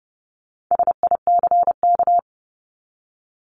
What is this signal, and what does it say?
Morse code, heard as a keyed tone, about 30 wpm, 700 Hz. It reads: SICK
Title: Top 1501-1750 Words 30wpm